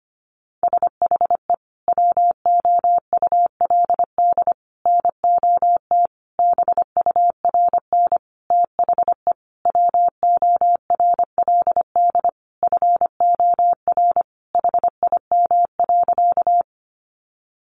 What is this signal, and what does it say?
Morse code, heard as a keyed tone, about 25 wpm, 700 Hz. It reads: SHE WOULD NOT BURN THE WORLD FOR HIM.
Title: Sentences from Top 1300 Words 25wpm